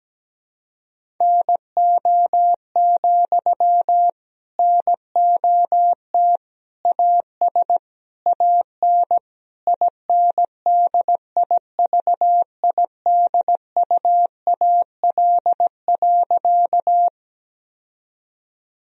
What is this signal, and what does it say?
Morse code, heard as a keyed tone, about 17 wpm, 700 Hz. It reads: NO, NOT AS AN INDIVIDUAL.
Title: Sentences from Top 1000 Words - Review Part 4 - 17wpm